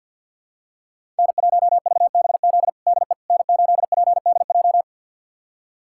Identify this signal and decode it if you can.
Morse code, heard as a keyed tone, about 49 wpm, 700 Hz. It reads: N1VBZ DE N8PDJ
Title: USA Callsign DE Callsign - 5 Characters - 50wpm